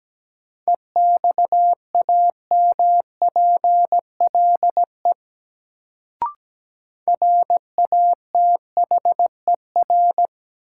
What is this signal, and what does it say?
Morse code, heard as a keyed tone, about 17 wpm, 700 Hz. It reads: EXAMPLE RATHER
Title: Sets of 2 Words - New Word Drill - 17wpm